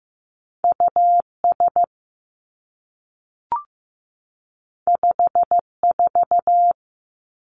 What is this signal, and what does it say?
Morse code, heard as a keyed tone, about 15 wpm, 700 Hz. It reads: U S 5 4
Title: Two Challenging Letter-Number Combinations 15wpm